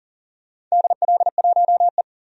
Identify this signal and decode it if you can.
Morse code, heard as a keyed tone, about 40 wpm, 700 Hz. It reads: DL1E
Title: DXCC-100 Call Signs - 4 Characters - 40wpm